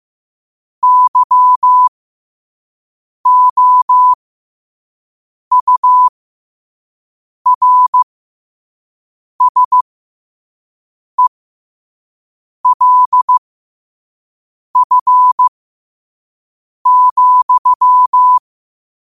Morse code, heard as a keyed tone, about 15 wpm, spaced as for 5 wpm, 1000 Hz. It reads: YOURSELF,
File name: qotdfiles / 5wpm